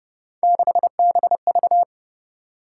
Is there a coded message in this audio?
Yes: Morse code from 0.4 to 1.8 s, heard as a keyed tone, about 29 wpm, 700 Hz. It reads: 6BV